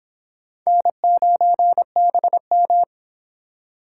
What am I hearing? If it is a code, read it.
Morse code, heard as a keyed tone, about 26 wpm, 700 Hz. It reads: N9BM